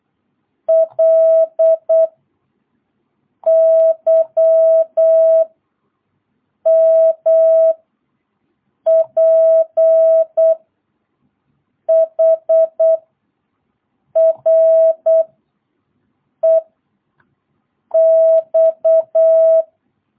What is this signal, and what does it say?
Morse code, heard as a keyed tone, about 8 wpm, 650 Hz. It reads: L Y M P H R E X
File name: jh8cbh / 2022cw